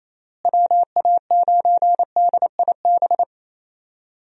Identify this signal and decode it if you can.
Morse code, heard as a keyed tone, about 28 wpm, 700 Hz. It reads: WA9DIB
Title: USA Call Signs - 6 Characters - Repeated 2x - 28wpm